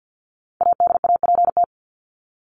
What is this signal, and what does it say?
Morse code, heard as a keyed tone, about 50 wpm, 700 Hz. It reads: ADAPT
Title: Top 1501-1750 Words 50wpm